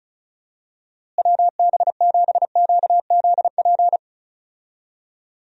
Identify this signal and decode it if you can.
Morse code, heard as a keyed tone, about 34 wpm, 700 Hz. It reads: WB7QZP
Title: USA Call Signs - 6 Characters - Repeated 2x - 35wpm